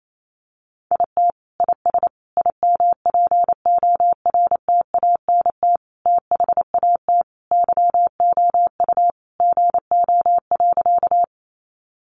Morse code, heard as a keyed tone, about 28 wpm, 700 Hz. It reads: IT IS IMPORTANT THAT YOU GO.
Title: Sentences from Top 600 Words 28wpm